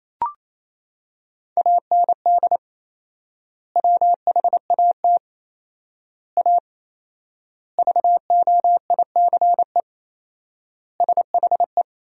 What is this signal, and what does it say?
Morse code, heard as a keyed tone, about 28 wpm, 700 Hz. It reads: AND WHAT A VOICE SHE
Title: 4x Word-Spacing - Sentences from Top 500 Words - 28wpm